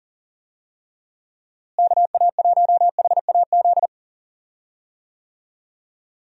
Morse code, heard as a keyed tone, about 40 wpm, 700 Hz. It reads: KA1HAZ
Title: USA Call Signs - 6 Characters - Repeated 2x - 40wpm